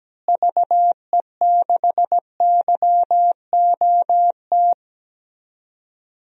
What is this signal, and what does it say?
Morse code, heard as a keyed tone, about 17 wpm, 700 Hz. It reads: VE6YOT
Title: DXCC-100 Call Signs - 6 Characters - Repeated 3x - 17wpm